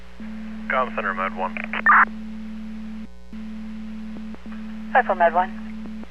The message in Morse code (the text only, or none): none